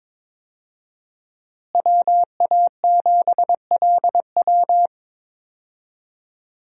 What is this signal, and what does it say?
Morse code, heard as a keyed tone, about 22 wpm, 700 Hz. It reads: WA7LW
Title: USA Call Signs - 5 Characters - Repeated 3x - 22wpm